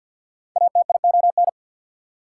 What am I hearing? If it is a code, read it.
Morse code, heard as a keyed tone, about 54 wpm, 690 Hz. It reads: ATION
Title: Prefixes and Suffixes 50wpm